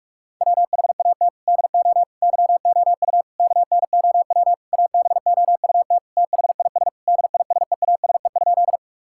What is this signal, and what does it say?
Morse code, heard as a keyed tone, about 46 wpm, 700 Hz. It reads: WHAT DO YOU KNOW ABOUT THIS DISEASE?
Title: Sentences from Top 1000 Words - Review Part 6 - 45wpm